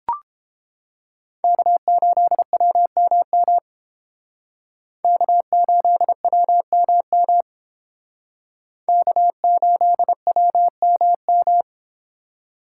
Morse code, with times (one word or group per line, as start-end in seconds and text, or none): 1.4-3.6 K8WMM
5.0-7.4 K8WMM
8.9-11.6 K8WMM